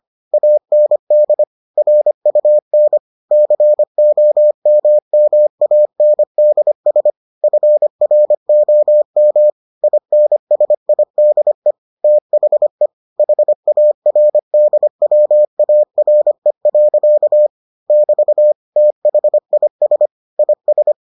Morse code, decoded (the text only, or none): AND RUN COMMANDS FROM INSIDE THE HARDWARE. <BT> THIS IS